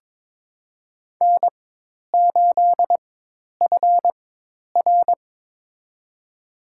Morse code, heard as a keyed tone, about 22 wpm, spaced as for 10 wpm, 700 Hz. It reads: N8FR
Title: USA Call Signs - 4 Characters - 4x Character Spacing - 22wpm